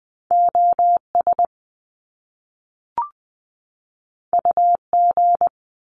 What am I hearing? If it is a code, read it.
Morse code, heard as a keyed tone, about 20 wpm, 700 Hz. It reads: OS UG
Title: Top Two Letter Combinations in Words 20wpm